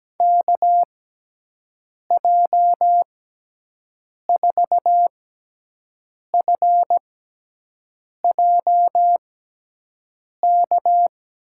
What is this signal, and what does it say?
Morse code, heard as a keyed tone, about 17 wpm, spaced as for 6 wpm, 700 Hz. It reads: KJ4FJK
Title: USA Call Signs - 6 Characters - 6x Character Spacing - 17wpm